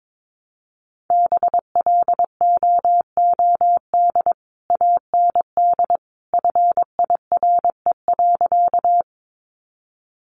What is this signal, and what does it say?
Morse code, heard as a keyed tone, about 22 wpm, 700 Hz. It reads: BLOOD AND FIRE.
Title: Sentences from Top 1100 Words 22wpm